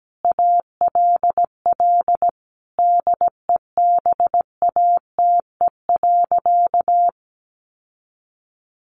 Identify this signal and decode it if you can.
Morse code, heard as a keyed tone, about 17 wpm, 700 Hz. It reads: ALL DEBATE.
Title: Sentences from Top 1100 Words 17wpm